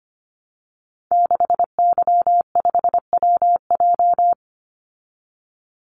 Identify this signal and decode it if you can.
Morse code, heard as a keyed tone, about 25 wpm, 700 Hz. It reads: 6Y5WJ
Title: Call Signs Repeated 3x - 25wpm